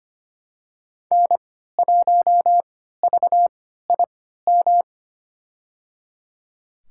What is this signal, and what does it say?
Morse code, heard as a keyed tone, about 25 wpm, spaced as for 14 wpm, 700 Hz. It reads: N1VIM
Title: USA Call Signs - 5 Characters - 3x Character Spacing - 25wpm